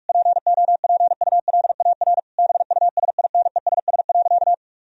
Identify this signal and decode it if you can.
Morse code, heard as a keyed tone, about 46 wpm, 700 Hz. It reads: POPULAR BUSINESS.
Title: Sentences from Top 1000 Words - Review Part 3 - 45wpm